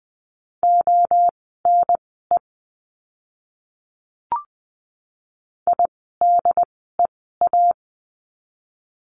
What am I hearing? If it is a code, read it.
Morse code, heard as a keyed tone, about 20 wpm, spaced as for 14 wpm, 700 Hz. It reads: ONE IDEA
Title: 108 Lesson 18f - Letter m - Words 20wpm